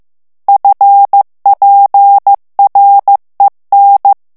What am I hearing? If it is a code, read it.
Morse code, heard as a keyed tone, about 15 wpm, 800 Hz. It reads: FPREN
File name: ALLEMAND texte clair 1 vitesse 14 mots minute